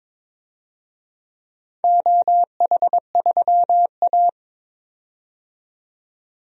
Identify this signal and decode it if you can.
Morse code, heard as a keyed tone, about 22 wpm, 700 Hz. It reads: OH3A
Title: DXCC-100 Call Signs - 4 Characters - 22wpm